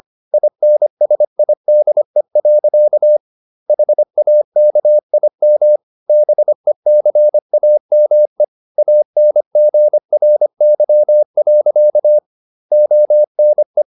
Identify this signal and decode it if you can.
Morse code, heard as a keyed tone, about 25 wpm, 600 Hz. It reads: INSIDE. HAKIM BECAME ANGRY. ONE
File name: Never 0025wpm